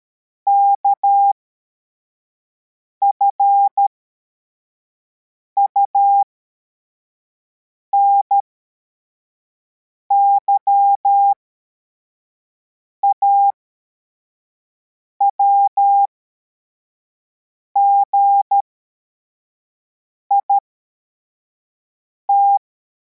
Morse code, heard as a keyed tone, about 13 wpm, spaced as for 4 wpm, 800 Hz. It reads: KFUNY AWGIT
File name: formation / cw / Text62